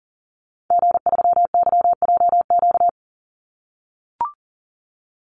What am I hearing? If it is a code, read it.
Morse code, heard as a keyed tone, about 40 wpm, 700 Hz. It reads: G3YJQ
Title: Call Signs - 40wpm